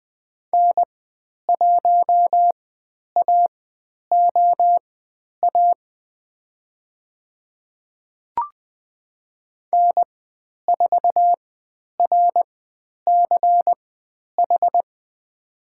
Morse code, heard as a keyed tone, about 20 wpm, spaced as for 10 wpm, 700 Hz. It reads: N1AOA N4RCH
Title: 102 Lesson 17g - Letter u - Call Signs 20wpm